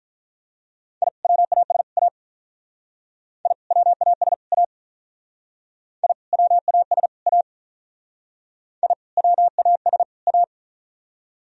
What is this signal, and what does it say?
Morse code, heard as a keyed tone, about 45 wpm, 700 Hz. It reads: I WAS A I WAS A I WAS A I WAS A